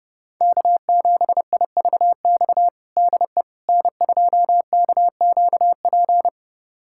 Morse code, heard as a keyed tone, about 30 wpm, 700 Hz. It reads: K7IVX DE N2KQP
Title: USA Callsign DE Callsign - 5 Characters - 30wpm